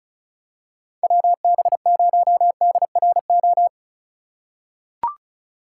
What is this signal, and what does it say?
Morse code, heard as a keyed tone, about 34 wpm, 700 Hz. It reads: WB0DRO
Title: USA Call Signs - 2x Character Spacing - 35wpm